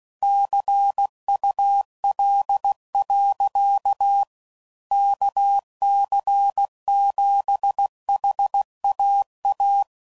Morse code, heard as a keyed tone, about 16 wpm, 800 Hz. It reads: CUL. KC7HAA
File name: CW Code Groups @15WPM / QSOs / kc7haa